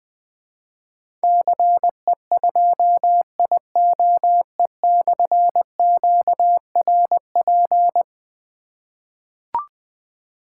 Morse code, heard as a keyed tone, about 21 wpm, 700 Hz. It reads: CE2IOE/QRP
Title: Difficult Call Signs 20wpm